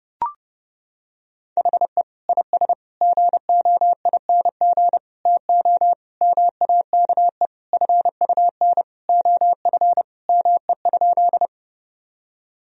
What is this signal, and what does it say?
Morse code, heard as a keyed tone, about 30 wpm, 700 Hz. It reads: HE IS GOING TO MAKE FUN OF ME?
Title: Sentences from Top 1000 Words - Review Part 4 - 30wpm